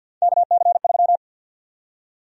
Morse code, heard as a keyed tone, about 53 wpm, 700 Hz. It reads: KC3
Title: USA Call Sign Prefixes - 50wpm